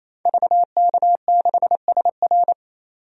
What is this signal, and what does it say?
Morse code, heard as a keyed tone, about 28 wpm, 700 Hz. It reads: VK6SR
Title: DXCC-100 Call Signs - 5 Characters - 28wpm